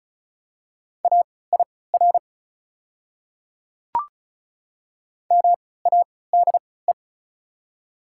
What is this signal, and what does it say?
Morse code, heard as a keyed tone, about 36 wpm, spaced as for 20 wpm, 700 Hz. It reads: AIR MADE
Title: 3x Character Spacing - Top 300 Words - 35wpm